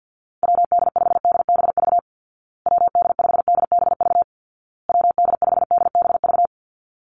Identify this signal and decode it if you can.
Morse code, heard as a keyed tone, about 50 wpm, 700 Hz. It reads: WD5DBV WD5DBV WD5DBV
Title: Call Signs Repeated 2x - 50wpm